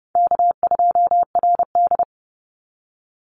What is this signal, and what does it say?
Morse code, heard as a keyed tone, about 30 wpm, 700 Hz. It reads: K2RD
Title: Easy Call Signs Repeated 3x - 30wpm